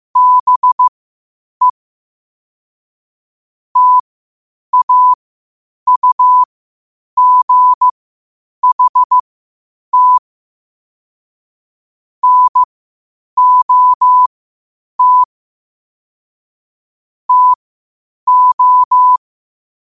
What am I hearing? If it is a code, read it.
Morse code, heard as a keyed tone, about 15 wpm, spaced as for 8 wpm, 1000 Hz. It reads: BE TAUGHT NOT TO
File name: qotdfiles / 8wpm